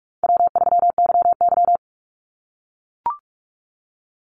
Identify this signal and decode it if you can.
Morse code, heard as a keyed tone, about 44 wpm, 700 Hz. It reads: W3YY
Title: Easy Call Signs - 45wpm